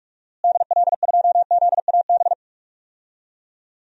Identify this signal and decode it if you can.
Morse code, heard as a keyed tone, about 46 wpm, 700 Hz. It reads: DL2ZAB
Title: DXCC-100 Call Signs - 6 Characters - Repeated 3x - 45wpm